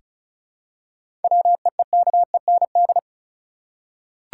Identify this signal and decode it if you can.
Morse code, heard as a keyed tone, about 36 wpm, 700 Hz. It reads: WEEKEND
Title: Seven Letter Words - 35wpm